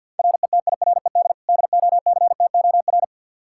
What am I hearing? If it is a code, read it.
Morse code, heard as a keyed tone, about 53 wpm, 700 Hz. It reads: RETIRED DOCTOR
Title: Practice QSO - Exchange 3 - More Info - 50wpm